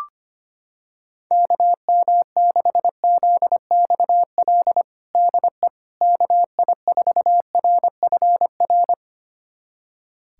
Morse code, heard as a keyed tone, about 25 wpm, 700 Hz. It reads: KM6ZXL DE KI4RFR
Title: USA Callsign DE Callsign 25wpm